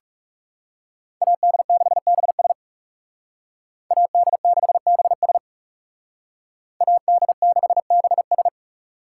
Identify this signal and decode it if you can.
Morse code, heard as a keyed tone, about 36 wpm, 700 Hz. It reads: AD6BS AD6BS AD6BS
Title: Speed Racing - USA Call Signs 30wpm